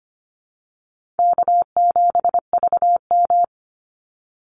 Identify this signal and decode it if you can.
Morse code, heard as a keyed tone, about 25 wpm, 700 Hz. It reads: K7VM
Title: Easy Call Signs - 25wpm